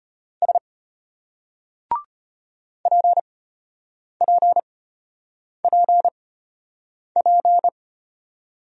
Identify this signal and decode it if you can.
Morse code, heard as a keyed tone, about 34 wpm, spaced as for 8 wpm, 700 Hz. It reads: S PPPP